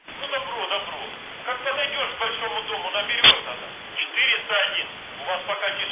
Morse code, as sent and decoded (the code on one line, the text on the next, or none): none